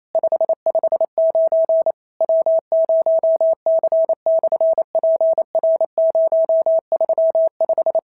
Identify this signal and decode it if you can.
Morse code, heard as a keyed tone, about 28 wpm, 650 Hz. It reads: 559 W0C/PR035
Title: SOTA QSO Practice 28wpm